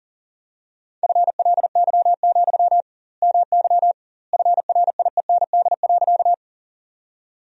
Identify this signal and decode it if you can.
Morse code, heard as a keyed tone, about 40 wpm, 700 Hz. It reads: FLY, MY FRIEND.
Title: 2x Word-Spacing - Sentences from Top 500 Words - 40wpm